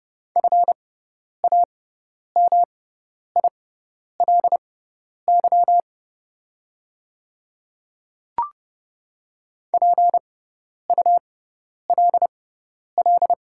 Morse code, heard as a keyed tone, about 30 wpm, spaced as for 10 wpm, 700 Hz. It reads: FAMILY PULL
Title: 6x Character Spacing - Top 500 Words - 30wpm